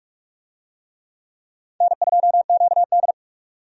Morse code, heard as a keyed tone, about 42 wpm, 700 Hz. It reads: N2QD
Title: USA Call Signs - 4 Characters - 45wpm